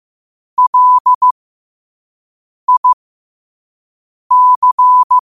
Morse code, heard as a keyed tone, about 15 wpm, spaced as for 5 wpm, 1000 Hz. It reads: LIC
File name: qotdfiles / 5wpm